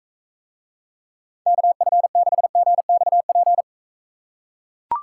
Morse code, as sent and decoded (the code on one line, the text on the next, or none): -.- ..-. -.... --. -..- .--.
KF6GXP